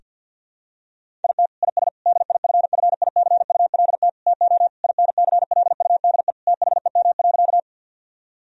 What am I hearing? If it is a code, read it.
Morse code, heard as a keyed tone, about 50 wpm, 700 Hz. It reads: IT IS DIFFICULT TO INCLUDE THEM.
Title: Sentences from Top 1000 Words - Review Part 3 - 50wpm